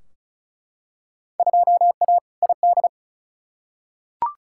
Morse code, heard as a keyed tone, about 34 wpm, 700 Hz. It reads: 2A ID
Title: Field Day Exchanges Encore - 35wpm